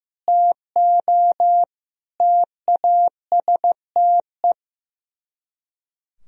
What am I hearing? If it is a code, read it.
Morse code, heard as a keyed tone, about 15 wpm, 700 Hz. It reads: TO TASTE
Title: Speed-Racing - Sets of 5 Words 15wpm